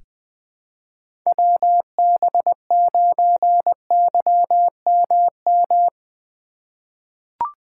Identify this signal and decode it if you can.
Morse code, heard as a keyed tone, about 20 wpm, 700 Hz. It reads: WB9YMM
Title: USA Call Signs - 6 Characters - 4x Character Spacing - 20wpm